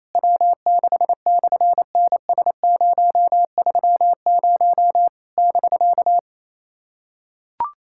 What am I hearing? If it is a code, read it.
Morse code, heard as a keyed tone, about 27 wpm, 700 Hz. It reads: W6/NS030 <BK>
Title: SOTA QSO Practice 28wpm